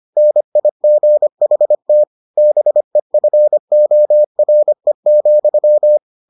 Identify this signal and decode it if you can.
Morse code, heard as a keyed tone, about 25 wpm, 600 Hz. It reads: NIGHT BEFORE,